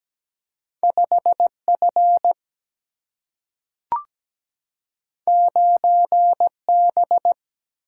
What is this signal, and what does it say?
Morse code, heard as a keyed tone, about 17 wpm, 700 Hz. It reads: 5 F 9 B
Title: Single-Letter Single-Number 17wpm